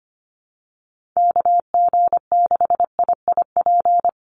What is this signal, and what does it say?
Morse code, heard as a keyed tone, about 25 wpm, 700 Hz. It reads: KG6IIP
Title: Call Signs Repeated 2x - 25wpm